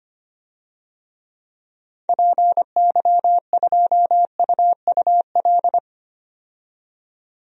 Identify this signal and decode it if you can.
Morse code, heard as a keyed tone, about 25 wpm, 700 Hz. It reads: PY2UUL
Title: DXCC-100 Call Signs - 6 Characters - 25wpm